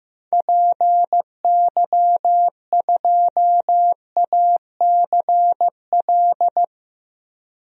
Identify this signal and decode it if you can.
Morse code, heard as a keyed tone, about 15 wpm, 700 Hz. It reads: PY2ACL
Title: DXCC-100 Call Signs - 6 Characters - 15wpm